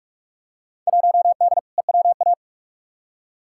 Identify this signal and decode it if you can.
Morse code, heard as a keyed tone, about 46 wpm, 700 Hz. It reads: 1D EWA
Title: Field Day Exchanges - 45wpm